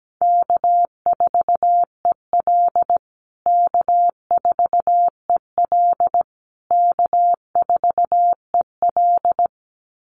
Morse code, heard as a keyed tone, about 17 wpm, 700 Hz. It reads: K4EL K4EL K4EL
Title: USA Call Signs Repeated 3x 17wpm